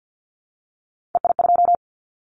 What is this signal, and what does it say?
Morse code, heard as a keyed tone, about 50 wpm, 700 Hz. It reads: EI2
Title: Callsign Prefixes for Top 50 DXCC - 50wpm